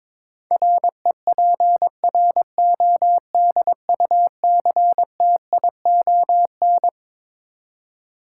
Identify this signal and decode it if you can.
Morse code, heard as a keyed tone, about 22 wpm, 700 Hz. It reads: REPRODUCTION
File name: Twelve Letter Words - 22wpm